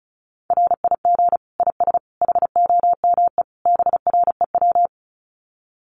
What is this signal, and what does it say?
Morse code, heard as a keyed tone, about 34 wpm, 700 Hz. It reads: RIG IS HOME BREW
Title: Common QSO Phrases 35wpm